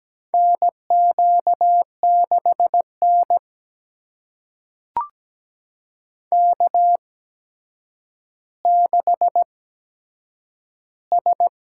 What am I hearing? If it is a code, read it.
Morse code, heard as a keyed tone, about 17 wpm, spaced as for 8 wpm, 700 Hz. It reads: NQ6N K6S
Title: USA Call Signs - 8x Character Spacing - 17wpm